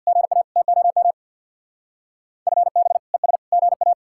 Morse code, heard as a keyed tone, about 62 wpm, 700 Hz. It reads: GA TOM FB ES GA
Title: Speed-Racing Common QSO Phrases Encore - 50wpm